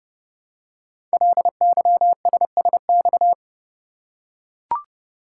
Morse code, heard as a keyed tone, about 30 wpm, 700 Hz. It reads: LYSSX
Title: Code Groups - 5 Letters - 30wpm